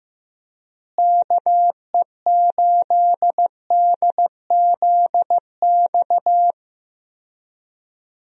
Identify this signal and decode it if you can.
Morse code, heard as a keyed tone, about 16 wpm, 700 Hz. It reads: KE8DZX
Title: USA Call Signs - 6 Characters - Repeated 3x - 15wpm